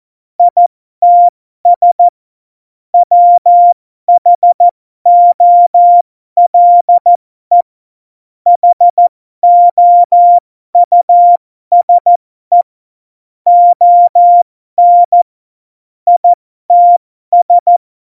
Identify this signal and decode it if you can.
Morse code, heard as a keyed tone, about 14 wpm, spaced as for 12 wpm, 700 Hz. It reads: ITS WHOLE HOUSE ON ITS
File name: Morsekurs / Witze